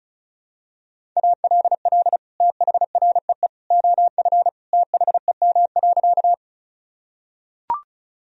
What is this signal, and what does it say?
Morse code, heard as a keyed tone, about 34 wpm, 700 Hz. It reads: ALL THREE OF THEM.
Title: Sentences from Top 1000 Words - Review Part 1 - 35wpm